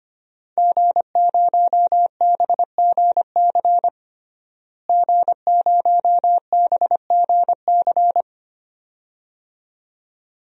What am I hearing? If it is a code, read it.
Morse code, heard as a keyed tone, about 25 wpm, 700 Hz. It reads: G0BGC G0BGC
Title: DXCC-100 Call Signs - 5 Characters - Repeated 2x - 25wpm